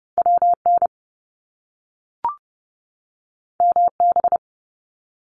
Morse code, heard as a keyed tone, about 30 wpm, 700 Hz. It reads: WN MB
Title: Top Two Letter Combinations in Words 30wpm